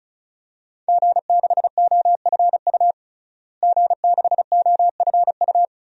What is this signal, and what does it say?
Morse code, heard as a keyed tone, about 36 wpm, 700 Hz. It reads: G6OFU G6OFU
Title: DXCC-100 Call Signs - 5 Characters - Repeated 2x - 35wpm